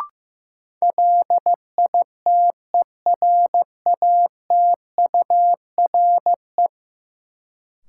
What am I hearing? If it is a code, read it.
Morse code, heard as a keyed tone, about 15 wpm, 700 Hz. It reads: LITERATURE
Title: Sentences from Top 600 Words - New Word Drill - 15wpm